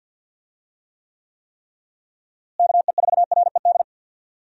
Morse code, heard as a keyed tone, about 52 wpm, 700 Hz. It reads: KE4RED
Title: USA Call Signs - 6 Characters - Repeated 3x - 50wpm